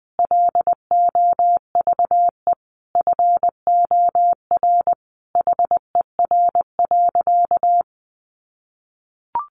Morse code, heard as a keyed tone, about 20 wpm, 700 Hz. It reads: LOVE FOR HER.
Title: Sentences from Top 1400 Words 20wpm